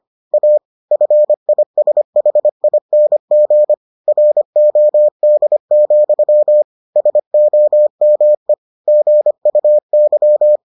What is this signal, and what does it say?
Morse code, heard as a keyed tone, about 25 wpm, 600 Hz. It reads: A FISHING ROD, SOME GUY